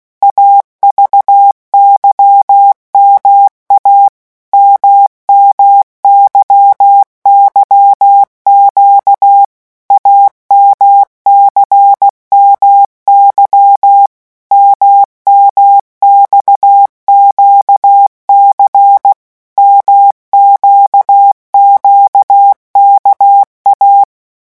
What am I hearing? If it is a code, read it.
Morse code, heard as a keyed tone, about 16 wpm, 800 Hz. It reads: AVYMA MMYYQ AMCMY MMXQC MQQKA
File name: morse mp3 / revision2-vitesse 15 mots